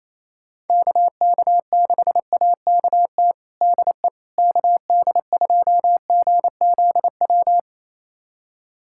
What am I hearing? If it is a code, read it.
Morse code, heard as a keyed tone, about 27 wpm, 700 Hz. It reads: KK6AKT DE KD2GZW
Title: USA Callsign DE Callsign - 6 Characters - 28wpm